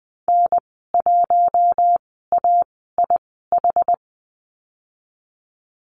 Morse code, heard as a keyed tone, about 20 wpm, spaced as for 14 wpm, 700 Hz. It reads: N1AIH
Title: 076 Lesson 13g - Letter L - Call Signs 20wpm